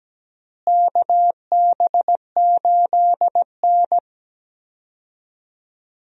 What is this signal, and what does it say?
Morse code, heard as a keyed tone, about 17 wpm, 700 Hz. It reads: KB8N
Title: USA Call Signs - 4 Characters - Repeated 3x - 17wpm